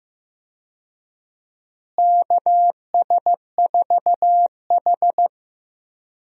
Morse code, heard as a keyed tone, about 15 wpm, 700 Hz. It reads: KS4H
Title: USA Call Signs - 4 Characters - Repeated 3x - 15wpm